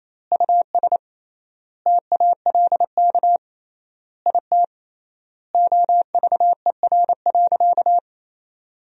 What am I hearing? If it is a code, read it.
Morse code, heard as a keyed tone, about 27 wpm, 700 Hz. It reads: US TALK IT OVER.
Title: 3x Word-Spacing - Sentences from Top 500 Words - 28wpm